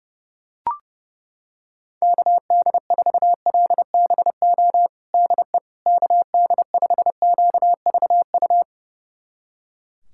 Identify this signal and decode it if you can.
Morse code, heard as a keyed tone, about 31 wpm, 700 Hz. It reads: KD4LBO DE KD5QVU
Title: USA Callsign DE Callsign - 6 Characters - 30wpm